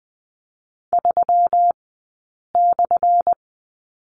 Morse code, heard as a keyed tone, about 20 wpm, spaced as for 8 wpm, 700 Hz. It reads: 3/
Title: 211 Lesson 34d - Number 8 - Two Characters 20wpm